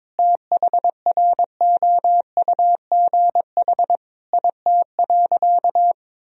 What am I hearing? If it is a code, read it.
Morse code, heard as a keyed tone, about 21 wpm, 700 Hz. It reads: THROUGH IT.
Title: Sentences from Top 1000 Words - Review Part 1 - 22wpm